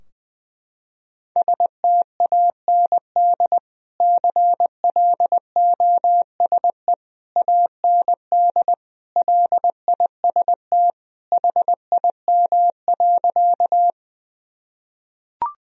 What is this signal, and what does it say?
Morse code, heard as a keyed tone, about 21 wpm, 700 Hz. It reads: STAND CLOSE AND LIST HIM.
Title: Sentences from Top 1000 Words - Review Part 2 - 20wpm